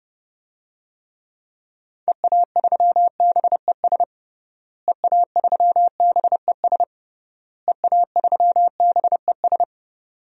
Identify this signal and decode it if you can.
Morse code, heard as a keyed tone, about 32 wpm, 700 Hz. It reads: EA3BES EA3BES EA3BES
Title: DXCC-100 Call Signs - 6 Characters - Repeated 3x - 30wpm